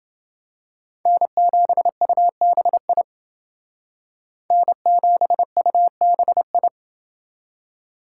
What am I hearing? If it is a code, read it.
Morse code, heard as a keyed tone, about 29 wpm, 700 Hz. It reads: N7UBI N7UBI